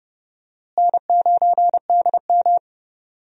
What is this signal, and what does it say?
Morse code, heard as a keyed tone, about 31 wpm, 700 Hz. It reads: N9DM
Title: USA Call Signs - 4 Characters - Repeated 2x - 30wpm